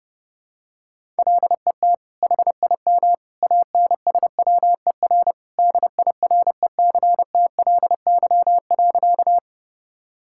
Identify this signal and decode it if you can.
Morse code, heard as a keyed tone, about 30 wpm, 700 Hz. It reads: LET HIM ANSWER DIRECTLY.
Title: Sentences from Top 1000 Words - Review Part 5 - 30wpm